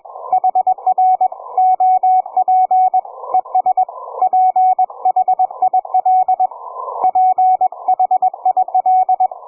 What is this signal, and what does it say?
Morse code, heard as a keyed tone, about 21 wpm, 740 Hz. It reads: HR OP ES PHIL PHIL
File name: cw forum / ve1zj 1830 160m 04h00